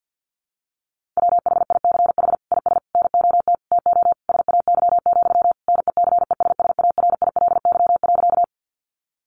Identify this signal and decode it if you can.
Morse code, heard as a keyed tone, about 50 wpm, 700 Hz. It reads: WHICH IS NOT TO SAY, NECESSARILY.
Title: Sentences from Top 1400 Words 50wpm